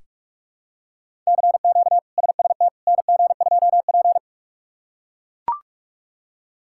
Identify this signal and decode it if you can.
Morse code, heard as a keyed tone, about 41 wpm, 700 Hz. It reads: CQ SST NG2P
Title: K1USN SST QSO Practice - 45wpm